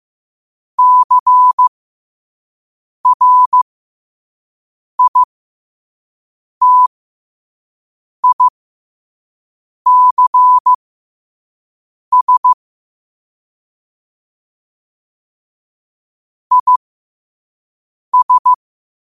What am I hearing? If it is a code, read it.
Morse code, heard as a keyed tone, about 15 wpm, spaced as for 5 wpm, 1000 Hz. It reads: CRITICS IS
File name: qotdfiles / 5wpm